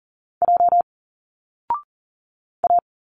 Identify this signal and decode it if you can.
Morse code, heard as a keyed tone, about 40 wpm, 700 Hz. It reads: J A
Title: Single Letters 40wpm